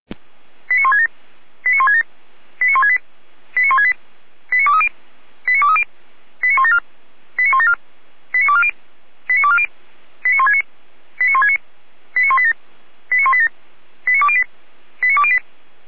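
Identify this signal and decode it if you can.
Five-tone selective call (ZVEI), heard as quick runs of five tones, tones of about 70 ms. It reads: 89167 89167 89168 89168 89230 89230 89174 89174 89240 89240 89169 89169 89187 89187 89208 89208
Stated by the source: ZVEI five-tone selective call received on an SDR.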